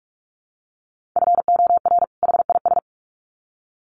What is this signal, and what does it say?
Morse code, heard as a keyed tone, about 44 wpm, 700 Hz. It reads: FOR HIS
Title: Sets of 2 Words 45wpm